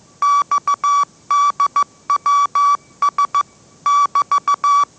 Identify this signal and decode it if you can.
Morse code, heard as a keyed tone, about 14 wpm, 1150 Hz. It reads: XDWS <BT>
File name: mrx / mrx065 16wpm